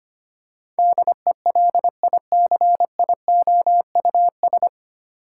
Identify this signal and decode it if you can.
Morse code, heard as a keyed tone, about 25 wpm, 700 Hz. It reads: DELICIOUS
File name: practice / 25wpm / 9 Letter Words - First 3 Letters Given - 25wpm